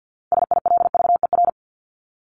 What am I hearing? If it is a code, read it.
Morse code, heard as a keyed tone, about 50 wpm, 700 Hz. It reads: SILVER
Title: Top 1751-2000 Words 50wpm